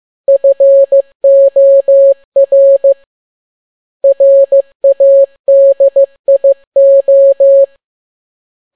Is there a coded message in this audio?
Yes: Morse code from 0.3 to 7.6 s, heard as a keyed tone, about 15 wpm, 550 Hz. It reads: FOR RADIO